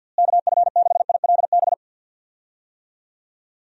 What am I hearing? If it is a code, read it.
Morse code, heard as a keyed tone, about 48 wpm, 700 Hz. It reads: KF6ILB
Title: USA Call Signs - 6 Characters - Repeated 2x - 50wpm